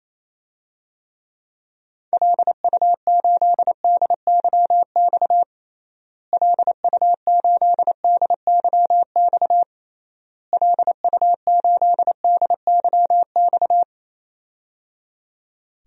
Morse code, heard as a keyed tone, about 27 wpm, 700 Hz. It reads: LU8DYX LU8DYX LU8DYX
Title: DXCC-100 Call Signs - 6 Characters - Repeated 3x - 28wpm